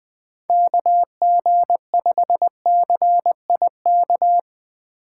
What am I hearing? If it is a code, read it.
Morse code, heard as a keyed tone, about 20 wpm, 700 Hz. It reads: KG5CIK
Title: 2023 POTA Hunter Call Signs - 20wpm